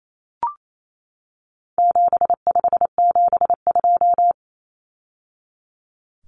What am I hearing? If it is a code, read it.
Morse code, heard as a keyed tone, about 28 wpm, 700 Hz. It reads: 7572
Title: Numbers - 4-Digit 28wpm